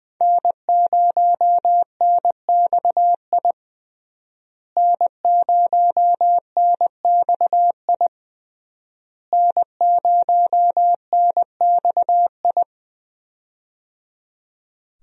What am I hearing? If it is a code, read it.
Morse code, heard as a keyed tone, about 20 wpm, 700 Hz. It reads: N0NXI N0NXI N0NXI
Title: USA Call Signs - 5 Characters - Repeated 3x - 20wpm